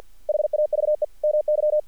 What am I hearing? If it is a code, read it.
Morse code, heard as a keyed tone, about 46 wpm, 600 Hz. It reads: HAVE MY
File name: archived / CRSnet